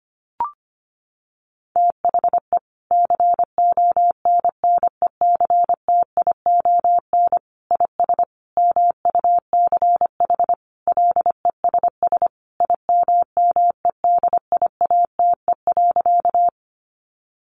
Morse code, heard as a keyed tone, about 25 wpm, 700 Hz. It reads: THE CONNECTION IS MUCH LESS IMMEDIATE.
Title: Sentences from Top 1300 Words 25wpm